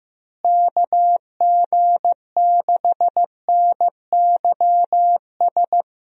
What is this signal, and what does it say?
Morse code, heard as a keyed tone, about 15 wpm, 700 Hz. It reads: KG6NYS
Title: USA Call Signs - 6 Characters - Repeated 3x - 15wpm